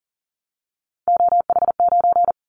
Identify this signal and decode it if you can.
Morse code, heard as a keyed tone, about 42 wpm, 700 Hz. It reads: OH9
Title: Callsign Prefixes for Top 50 DXCC - 40wpm